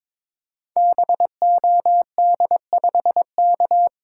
Morse code, heard as a keyed tone, about 21 wpm, 700 Hz. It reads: BOD5K
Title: Code Groups - 5 Letter:Numbers - 22wpm